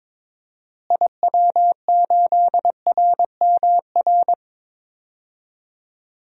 Morse code, heard as a keyed tone, about 22 wpm, 700 Hz. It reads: IW8RMR
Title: DXCC-100 Call Signs - 6 Characters - Repeated 3x - 22wpm